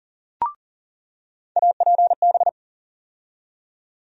Morse code, heard as a keyed tone, about 40 wpm, 700 Hz. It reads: APB
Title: USA Call Sign Postfixes - 3 Letters 40wpm